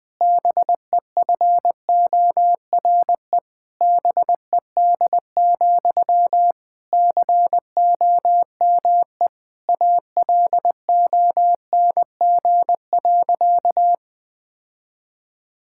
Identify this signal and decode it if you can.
Morse code, heard as a keyed tone, about 20 wpm, 700 Hz. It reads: BEFORE BED, COME ALONG.
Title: Sentences from Top 1600 Words 20wpm